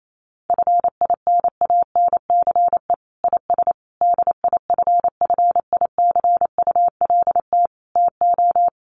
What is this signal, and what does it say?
Morse code, heard as a keyed tone, about 28 wpm, 700 Hz. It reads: FINANCE IS DIFFICULT TO
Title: Sentences from Top 1500 Words 28wpm